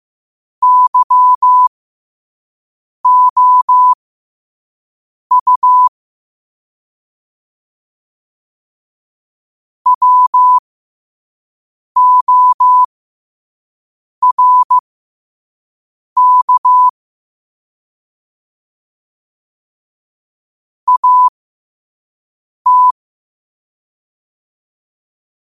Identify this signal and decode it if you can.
Morse code, heard as a keyed tone, about 15 wpm, spaced as for 5 wpm, 1000 Hz. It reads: YOU WORK AT
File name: qotdfiles / 5wpm